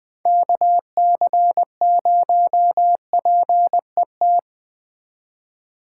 Morse code, heard as a keyed tone, about 21 wpm, 700 Hz. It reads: KC0PET